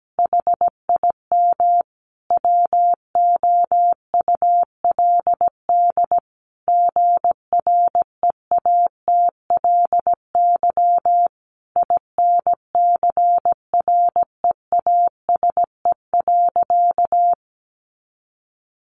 Morse code, heard as a keyed tone, about 17 wpm, 700 Hz. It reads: HIM WOULD GREATLY INCREASE.
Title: Sentences from Top 1400 Words 17wpm